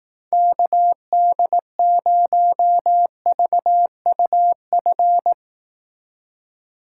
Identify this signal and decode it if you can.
Morse code, heard as a keyed tone, about 18 wpm, 700 Hz. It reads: KD0VUF
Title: Speed Racing - USA 2x3 Call Signs 15wpm